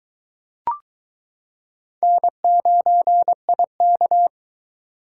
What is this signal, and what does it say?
Morse code, heard as a keyed tone, about 23 wpm, 700 Hz. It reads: N9IK